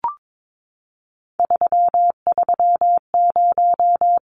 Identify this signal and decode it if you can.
Morse code, heard as a keyed tone, about 22 wpm, 700 Hz. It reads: 330
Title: Numbers - 3-Digit 22wpm